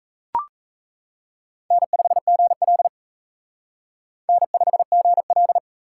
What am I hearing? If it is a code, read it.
Morse code, heard as a keyed tone, about 35 wpm, 700 Hz. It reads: N5GL N5GL
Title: Speed Racing - USA Call Signs 28wpm